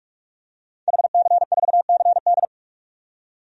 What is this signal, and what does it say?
Morse code, heard as a keyed tone, about 45 wpm, 700 Hz. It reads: HC4CD
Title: DXCC-100 Call Signs - 5 Characters - 45wpm